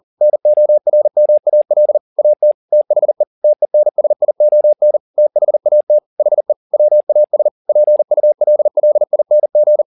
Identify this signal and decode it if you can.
Morse code, heard as a keyed tone, about 40 wpm, 600 Hz. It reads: NORMAL AT THE TENSION THAT HE WAS PULLING
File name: the old man and the sea 2 40wpm